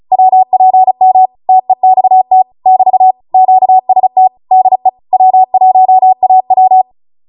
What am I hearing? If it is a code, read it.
Morse code, heard as a keyed tone, about 35 wpm, 750 Hz. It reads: WPM TEXT <BT> QST DE W1AW